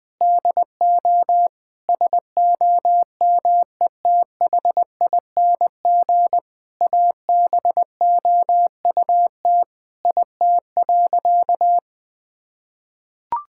Morse code, heard as a keyed tone, about 21 wpm, 700 Hz. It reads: DO SOMETHING ABOUT IT.
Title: Sentences with 5-Word Phrases - Spoken Beginning - 20wpm